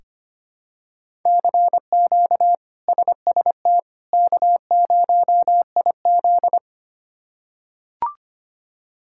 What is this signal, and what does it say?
Morse code, heard as a keyed tone, about 25 wpm, 700 Hz. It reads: CQ SST K0IZ
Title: K1USN SST QSO Practice - 25wpm